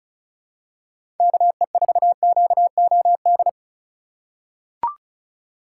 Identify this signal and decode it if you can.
Morse code, heard as a keyed tone, about 36 wpm, 700 Hz. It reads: KE4QOD
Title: USA Call Signs - 3x Character Spacing - 35wpm